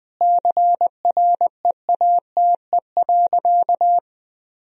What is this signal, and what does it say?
Morse code, heard as a keyed tone, about 20 wpm, 700 Hz. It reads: CREATE.
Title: Sentences from Top 1000 Words - Review Part 3 - 20wpm